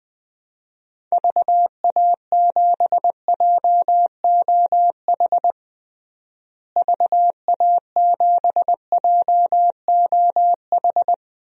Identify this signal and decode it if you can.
Morse code, heard as a keyed tone, about 21 wpm, 700 Hz. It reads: VA7JOH VA7JOH
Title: DXCC-100 Call Signs - 6 Characters - Repeated 2x - 20wpm